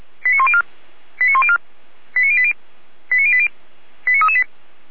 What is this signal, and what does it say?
Five-tone selective call (ZVEI), heard as quick runs of five tones, tones of about 70 ms. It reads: 89104 89104 89080 89080 89338